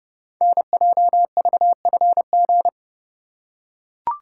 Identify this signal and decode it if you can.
Morse code, heard as a keyed tone, about 30 wpm, 700 Hz. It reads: NJVFG
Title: Code Groups - 5 Letters - 30wpm